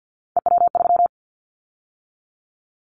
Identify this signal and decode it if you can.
Morse code, heard as a keyed tone, about 50 wpm, 700 Hz. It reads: EW3
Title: Callsign Prefixes for Top 50 DXCC - 50wpm